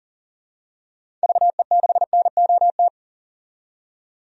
Morse code, heard as a keyed tone, about 40 wpm, 700 Hz. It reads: VE6NOT
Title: DXCC-100 Call Signs - 6 Characters - 40wpm